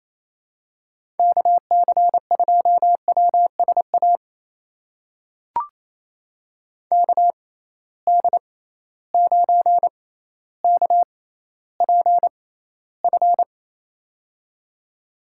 Morse code, heard as a keyed tone, about 28 wpm, spaced as for 14 wpm, 700 Hz. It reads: KC2WSA KD9KPF
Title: USA Call Signs - 6 Characters - 6x Character Spacing - 28wpm